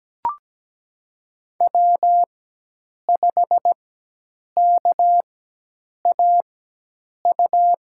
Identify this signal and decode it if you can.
Morse code, heard as a keyed tone, about 17 wpm, spaced as for 8 wpm, 700 Hz. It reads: W5KAU
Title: USA Call Signs - 5 Characters - 4x Character Spacing - 17wpm